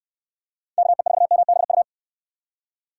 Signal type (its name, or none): Morse code